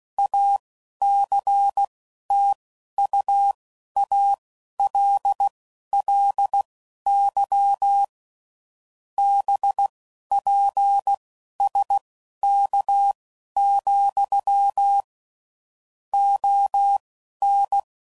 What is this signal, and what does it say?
Morse code, heard as a keyed tone, about 16 wpm, spaced as for 11 wpm, 800 Hz. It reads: ACTUALLY BPSK, ON
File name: CW Code Groups @10WPM / QSOs / QSO kc7haa